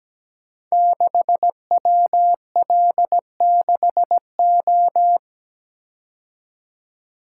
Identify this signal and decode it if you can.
Morse code, heard as a keyed tone, about 17 wpm, 700 Hz. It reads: 6WL6O
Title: Code Groups - 5 Letter:Numbers - 17wpm